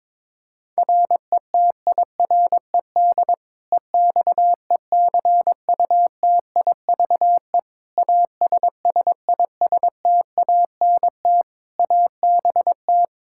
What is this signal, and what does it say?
Morse code, heard as a keyed tone, about 22 wpm, 700 Hz. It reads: RETIRED EXECUTIVE ASSISTANT ABT
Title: Practice QSO - Exchange 3 - More Info - 22wpm